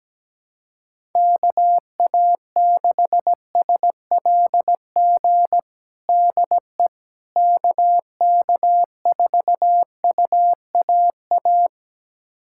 Morse code, heard as a keyed tone, about 17 wpm, 700 Hz. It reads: KA6SLG DE KK4UAA
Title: USA Callsign DE Callsign - 6 Characters - 17wpm